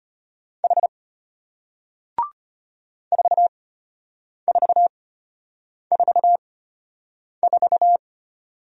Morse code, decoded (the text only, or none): H 4444